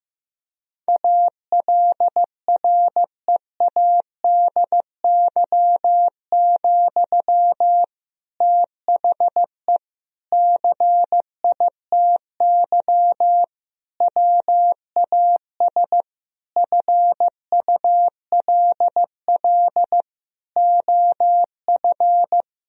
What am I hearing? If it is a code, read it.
Morse code, heard as a keyed tone, about 15 wpm, 700 Hz. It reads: ALREADY, THE CITY WAS FULL OF
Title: Sentences from Top 1000 Words - Review Part 4 - 15wpm